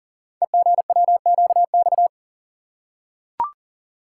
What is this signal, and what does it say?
Morse code, heard as a keyed tone, about 39 wpm, 700 Hz. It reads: EGWQX
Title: Code Groups - 5 Letters - 40wpm